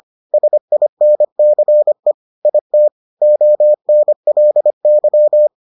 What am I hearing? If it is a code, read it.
Morse code, heard as a keyed tone, about 25 wpm, 600 Hz. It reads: SINCE IT ONLY